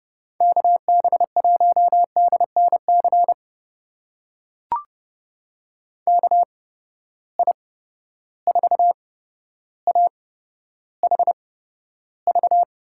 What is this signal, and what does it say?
Morse code, heard as a keyed tone, about 30 wpm, spaced as for 13 wpm, 700 Hz. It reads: KB1DNC KI4AHV
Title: USA Call Signs - 6 Characters - 8x Character Spacing - 30wpm